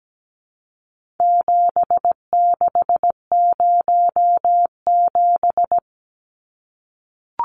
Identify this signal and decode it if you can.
Morse code, heard as a keyed tone, about 17 wpm, 700 Hz. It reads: 7607
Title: Numbers - 4-Digit 17wpm